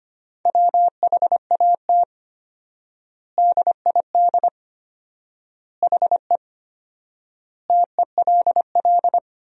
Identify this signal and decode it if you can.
Morse code, heard as a keyed tone, about 25 wpm, 700 Hz. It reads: WHAT DID HE TELL
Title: 4x Word-Spacing - Sentences from Top 200 Words - 25wpm